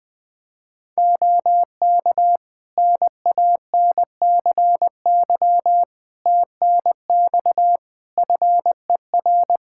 Morse code, heard as a keyed tone, about 20 wpm, 700 Hz. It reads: OK NANCY TNX FER
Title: Practice QSO - Exchange 4 - Ending - 20wpm